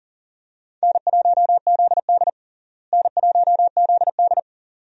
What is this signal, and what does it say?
Morse code, heard as a keyed tone, about 40 wpm, 700 Hz. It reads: N1ZD N1ZD
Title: USA Call Signs - 4 Characters - Repeated 2x - 40wpm